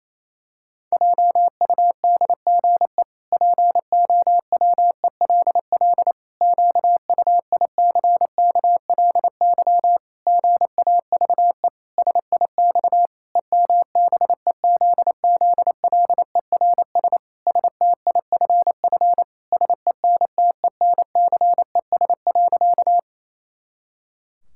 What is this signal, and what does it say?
Morse code, heard as a keyed tone, about 28 wpm, 700 Hz. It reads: JUDGE POWELL QUICKLY GAVE SIX EMBEZZLERS STIFF SENTENCES.
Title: Pangram Challenge v2 - 28wpm